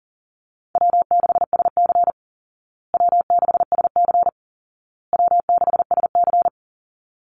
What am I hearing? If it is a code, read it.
Morse code, heard as a keyed tone, about 40 wpm, 700 Hz. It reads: W6SC W6SC W6SC
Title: Call Signs Repeated 2x - 40wpm